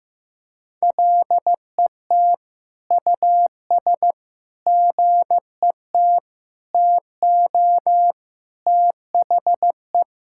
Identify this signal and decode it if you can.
Morse code, heard as a keyed tone, about 15 wpm, 700 Hz. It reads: LET US GET TO THE
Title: Sentences from Top 1600 Words 15wpm